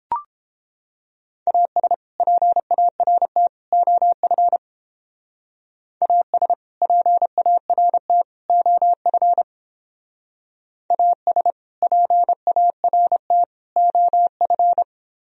Morse code, heard as a keyed tone, about 31 wpm, 700 Hz. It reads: AS PART OF AS PART OF AS PART OF